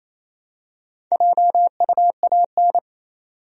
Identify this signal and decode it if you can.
Morse code, heard as a keyed tone, about 28 wpm, 700 Hz. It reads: JUAN
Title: USA Common Names - Max 4 Letters - 28wpm